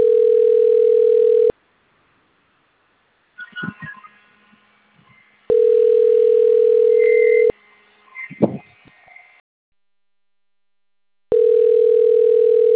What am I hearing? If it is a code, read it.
Ringback tone (North America), 440 + 480 Hz.